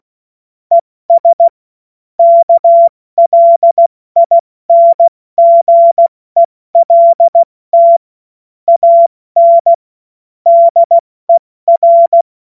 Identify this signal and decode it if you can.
Morse code, heard as a keyed tone, about 16 wpm, 680 Hz. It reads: ES KLINGELT AN DER
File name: Morsekurs / Witze / Witz 25. April 70 F80